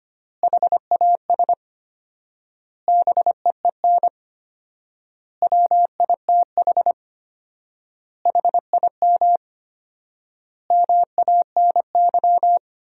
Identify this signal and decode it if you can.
Morse code, heard as a keyed tone, about 25 wpm, 700 Hz. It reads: HAS BEEN WITH HIM MANY
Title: 4x Word-Spacing - Sentences from Top 200 Words - 25wpm